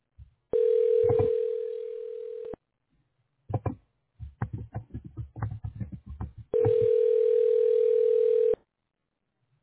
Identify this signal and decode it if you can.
Ringback tone (North America), 440 + 480 Hz.